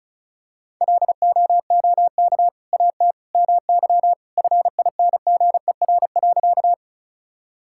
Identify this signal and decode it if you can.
Morse code, heard as a keyed tone, about 36 wpm, 700 Hz. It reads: LOOK AT MY FINGER.
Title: Sentences from Top 1600 Words 35wpm